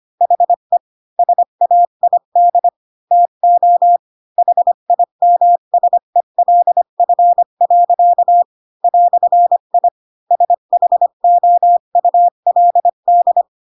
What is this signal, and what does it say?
Morse code, heard as a keyed tone, about 25 wpm, 700 Hz. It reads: HE SAID TO HIMSELF. "I SHOULD
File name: Pinocchio / Pinocchio ch3 20 25 ik1hge